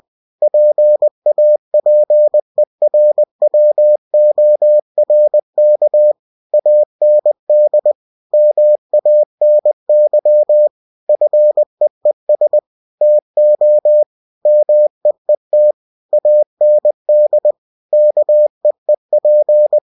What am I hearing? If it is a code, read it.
Morse code, heard as a keyed tone, about 20 wpm, 600 Hz. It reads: PAPERWORK AND MANY FEES TO MEET AND KEEP
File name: cw-books / winnie-the-pooh / winnie-20wpm-page-0121